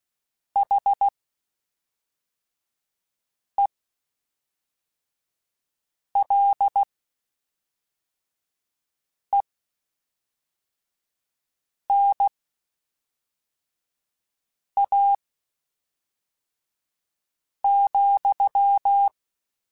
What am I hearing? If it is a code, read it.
Morse code, heard as a keyed tone, about 16 wpm, spaced as for 3 wpm, 800 Hz. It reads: HELENA,
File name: (spaced) / QSOs Spaced-mp3s / WD0BGP 1 @5 sp